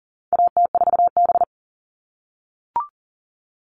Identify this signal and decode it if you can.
Morse code, heard as a keyed tone, about 40 wpm, 700 Hz. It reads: AT4B
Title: All Cut Numbers - 4-Digit 40wpm